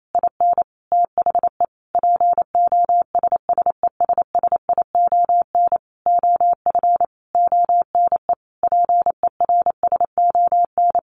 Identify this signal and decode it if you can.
Morse code, heard as a keyed tone, about 28 wpm, 700 Hz. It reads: IN THE POSSESSION OF ONE PERSON
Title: Sentences from Top 1400 Words 28wpm